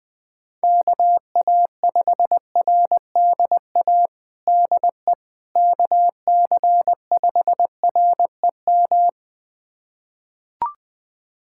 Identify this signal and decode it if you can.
Morse code, heard as a keyed tone, about 20 wpm, 700 Hz. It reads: KA5RDA DE KC5REM
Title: USA Callsign DE Callsign 20wpm